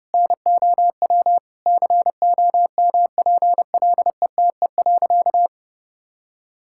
Morse code, heard as a keyed tone, about 30 wpm, 700 Hz. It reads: NOW COMPLETE.
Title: Sentences from Top 1000 Words - Review Part 6 - 30wpm